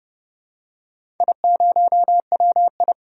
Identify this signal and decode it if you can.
Morse code, heard as a keyed tone, about 31 wpm, 700 Hz. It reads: I0WI
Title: DXCC-100 Call Signs - 4 Characters - 30wpm